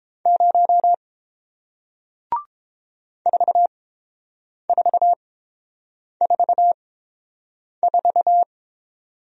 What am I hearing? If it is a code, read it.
Morse code, heard as a keyed tone, about 30 wpm, spaced as for 7 wpm, 700 Hz. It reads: 0 4444